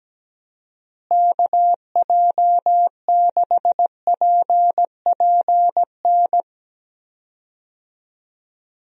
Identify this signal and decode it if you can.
Morse code, heard as a keyed tone, about 17 wpm, 700 Hz. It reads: KJ6PPN
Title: USA Call Signs - 6 Characters - Repeated 2x - 17wpm